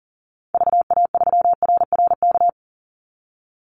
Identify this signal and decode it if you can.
Morse code, heard as a keyed tone, about 40 wpm, 700 Hz. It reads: VA3RRK
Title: Call Signs Repeated 3x - 40wpm